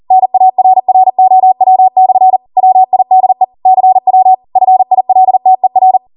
Morse code, heard as a keyed tone, about 40 wpm, 750 Hz. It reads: NARROW/ WIDE CW FILTER
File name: Morse / 40WPM